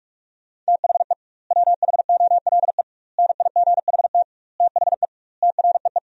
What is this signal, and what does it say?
Morse code, heard as a keyed tone, about 46 wpm, 700 Hz. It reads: THE WHOLE NIGHT THE TREE
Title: 2x Word-Spacing - Sentences from Top 500 Words - 45wpm